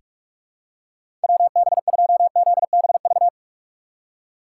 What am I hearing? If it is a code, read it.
Morse code, heard as a keyed tone, about 46 wpm, 700 Hz. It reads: WB2ZBV
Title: USA Call Signs - 6 Characters - 6x Character Spacing - 45wpm